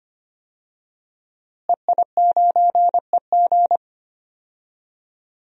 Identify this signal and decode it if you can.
Morse code, heard as a keyed tone, about 25 wpm, 700 Hz. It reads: EI9EG
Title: DXCC-100 Call Signs - 5 Characters - Repeated 3x - 25wpm